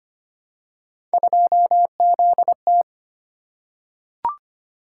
Morse code, heard as a keyed tone, about 25 wpm, 700 Hz. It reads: 2ZT
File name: Three Letter-Number Combinations - 25wpm